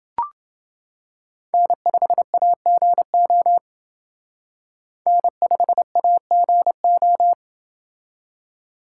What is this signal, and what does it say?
Morse code, heard as a keyed tone, about 28 wpm, 700 Hz. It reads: N5AGO N5AGO